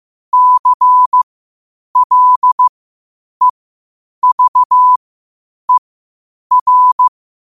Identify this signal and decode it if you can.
Morse code, heard as a keyed tone, about 15 wpm, spaced as for 8 wpm, 1000 Hz. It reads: CLEVER
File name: qotdfiles / 8wpm